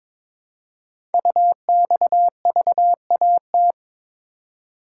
Morse code, heard as a keyed tone, about 22 wpm, 700 Hz. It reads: UXVAT
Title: Code Groups - 5 Letters - 22wpm